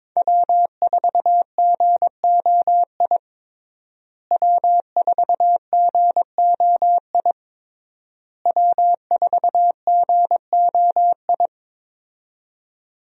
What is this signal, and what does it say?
Morse code, heard as a keyed tone, about 21 wpm, 700 Hz. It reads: W4GOI W4GOI W4GOI
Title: USA Call Signs - 5 Characters - Repeated 3x - 22wpm